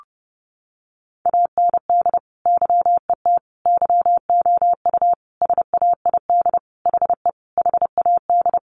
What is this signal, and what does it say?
Morse code, heard as a keyed tone, about 30 wpm, 700 Hz. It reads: AND YET YOU SAID HE HAD
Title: Sentences from Top 500 Words 30wpm